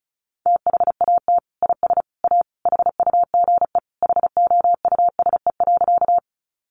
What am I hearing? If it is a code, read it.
Morse code, heard as a keyed tone, about 34 wpm, 700 Hz. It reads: THAT IS A HUGE HOUSE.
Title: Sentences from Top 1100 Words 35wpm